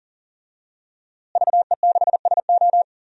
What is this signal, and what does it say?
Morse code, heard as a keyed tone, about 39 wpm, 700 Hz. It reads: VE6SO
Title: DXCC-100 Call Signs - 5 Characters - Repeated 3x - 40wpm